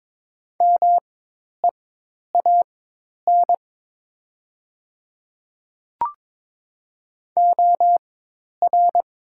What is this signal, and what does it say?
Morse code, heard as a keyed tone, about 22 wpm, spaced as for 10 wpm, 700 Hz. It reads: MEAN OR
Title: 4x Character Spacing - Top 200 Words - 22wpm